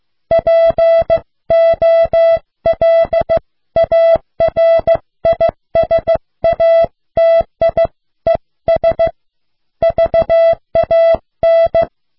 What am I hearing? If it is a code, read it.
Morse code, heard as a keyed tone, about 15 wpm, 660 Hz. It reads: POLARISATIES VAN